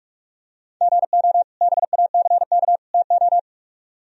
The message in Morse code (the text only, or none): GO BACK TO